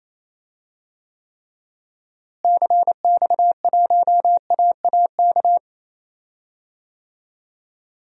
Morse code, heard as a keyed tone, about 28 wpm, 700 Hz. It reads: CX1AAK
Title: DXCC-100 Call Signs - 6 Characters - Repeated 2x - 28wpm